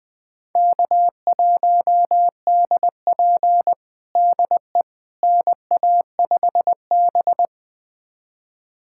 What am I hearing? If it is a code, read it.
Morse code, heard as a keyed tone, about 20 wpm, 700 Hz. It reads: K1DP DE NA5B
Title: USA Callsign DE Callsign - 4 Characters - 20wpm